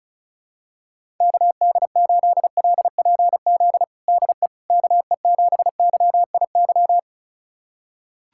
Morse code, heard as a keyed tone, about 36 wpm, 700 Hz. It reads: KD8LPZ DE KE7YIY
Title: USA Callsign DE Callsign - 6 Characters - 35wpm